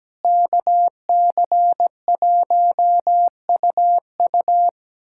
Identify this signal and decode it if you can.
Morse code, heard as a keyed tone, about 17 wpm, 700 Hz. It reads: KC1UU
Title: USA Callsign DE Callsign - 5 Characters - 17wpm